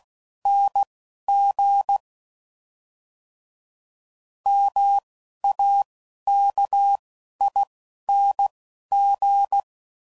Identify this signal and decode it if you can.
Morse code, heard as a keyed tone, about 16 wpm, spaced as for 11 wpm, 800 Hz. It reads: NG MAKING
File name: Suffixes / Special INGs / Single INGs